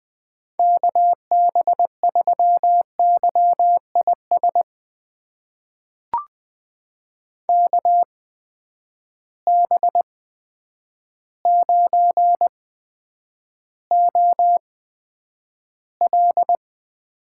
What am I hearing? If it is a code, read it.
Morse code, heard as a keyed tone, about 20 wpm, spaced as for 9 wpm, 700 Hz. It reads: KB3YIS KB9OL
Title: USA Call Signs - 8x Character Spacing - 20wpm